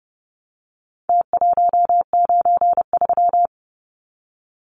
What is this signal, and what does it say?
Morse code, heard as a keyed tone, about 30 wpm, 700 Hz. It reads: T193
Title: Numbers - Contest Style - 4-Digit 30wpm